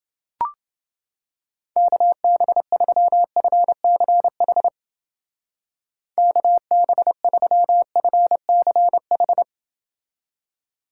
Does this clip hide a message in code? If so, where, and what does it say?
Yes: Morse code from 1.8 to 9.4 s, heard as a keyed tone, about 28 wpm, 700 Hz. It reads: KB3FCH KB3FCH